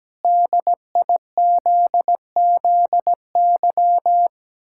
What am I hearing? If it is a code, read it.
Morse code, heard as a keyed tone, about 17 wpm, 700 Hz. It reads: DIZZY
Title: Pangram Challenge v2 - 17wpm